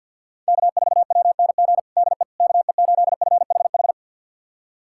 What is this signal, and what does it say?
Morse code, heard as a keyed tone, about 50 wpm, 700 Hz. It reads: K4WNG DE KE8FHH
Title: USA Callsign DE Callsign 50wpm